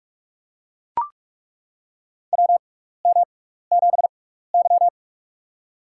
Morse code, heard as a keyed tone, about 42 wpm, spaced as for 15 wpm, 700 Hz. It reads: WM7Y
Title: USA Call Signs - 4 Characters - 6x Character Spacing - 45wpm